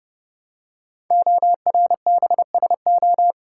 Morse code, heard as a keyed tone, about 29 wpm, 700 Hz. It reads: ORBSO
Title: Code Groups - 5 Letter:Numbers - 30wpm